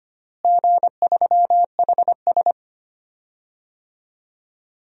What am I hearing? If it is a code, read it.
Morse code, heard as a keyed tone, about 25 wpm, 700 Hz. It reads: G3HS
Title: DXCC-100 Call Signs - 4 Characters - Repeated 3x - 25wpm